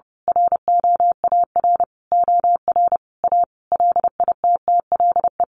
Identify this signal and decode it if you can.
Morse code, heard as a keyed tone, about 30 wpm, 700 Hz. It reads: ROAR OR A LITTLE
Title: A Sound Of Thunder 30wpm